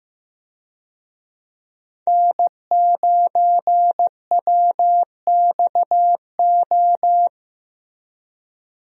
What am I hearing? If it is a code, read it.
Morse code, heard as a keyed tone, about 15 wpm, 700 Hz. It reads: N9WXO
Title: USA Call Signs - 5 Characters - Repeated 3x - 15wpm